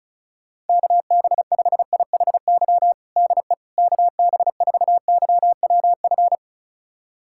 Morse code, heard as a keyed tone, about 36 wpm, 700 Hz. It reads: KB5IHY DE KB4YWF
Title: USA Callsign DE Callsign - 6 Characters - 35wpm